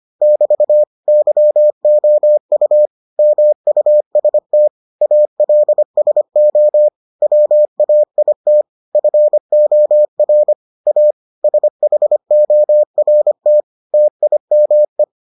Morse code, heard as a keyed tone, about 25 wpm, 600 Hz. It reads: <BT> YOU MUST ALSO WAIT FOR A SHORT TIME